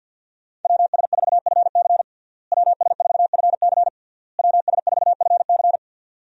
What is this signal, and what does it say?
Morse code, heard as a keyed tone, about 52 wpm, 700 Hz. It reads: WS4FC WS4FC WS4FC
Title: USA Call Signs - 5 Characters - Repeated 3x - 50wpm